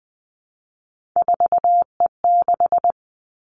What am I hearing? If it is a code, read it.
Morse code, heard as a keyed tone, about 20 wpm, 700 Hz. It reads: 4E6
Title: Three Challenging Letter-Number Combinations 20wpm